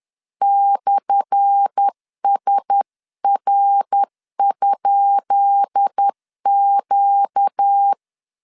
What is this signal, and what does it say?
Morse code, heard as a keyed tone, about 10 wpm, 800 Hz. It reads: /SR?Q
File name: LEZIONI / 10 SIGNS / 10B INT 10wpm 2844